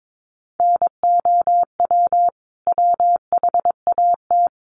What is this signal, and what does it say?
Morse code, heard as a keyed tone, about 22 wpm, 700 Hz. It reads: NOW WHAT
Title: Sentences from Top 100 Words 22wpm